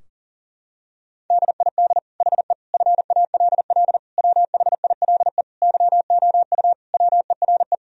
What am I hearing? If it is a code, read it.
Morse code, heard as a keyed tone, about 40 wpm, 700 Hz. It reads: DID HE FALL WHILE YOU WERE
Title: Sentences from Top 1000 Words - Review Part 2 - 40wpm